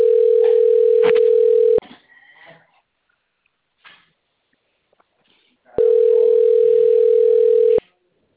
Ringback tone (North America), 440 + 480 Hz.